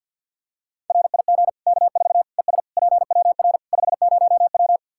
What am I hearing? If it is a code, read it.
Morse code, heard as a keyed tone, about 50 wpm, 700 Hz. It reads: RIG K4 ES PWR 50W
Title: K1USN SST QSO Practice 50wpm